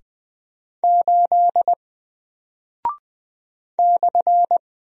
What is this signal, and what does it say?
Morse code, heard as a keyed tone, about 20 wpm, 700 Hz. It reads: 8 /
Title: Single Character (Letter:Number:Punctuation:Prosign) 20wpm